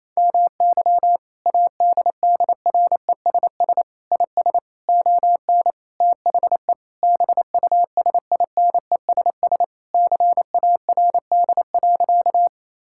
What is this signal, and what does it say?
Morse code, heard as a keyed tone, about 28 wpm, 700 Hz. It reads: MY ADDRESS IS ON THE BUSINESS CARD.
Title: Sentences from Top 1000 Words - Review Part 6 - 28wpm